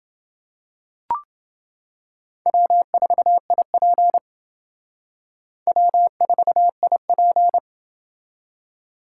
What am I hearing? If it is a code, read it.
Morse code, heard as a keyed tone, about 28 wpm, 700 Hz. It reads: W4IP W4IP